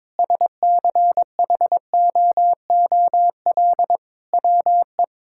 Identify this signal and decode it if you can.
Morse code, heard as a keyed tone, about 22 wpm, 700 Hz. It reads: SCHOOL WE
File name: Sentences with 9-Word Phrases - 22wpm